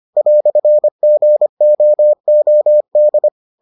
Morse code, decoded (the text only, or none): "GOOD